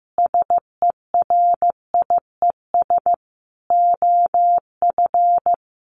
Morse code, heard as a keyed tone, about 15 wpm, 700 Hz. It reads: SERIES OF
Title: Sets of 3 Words 15wpm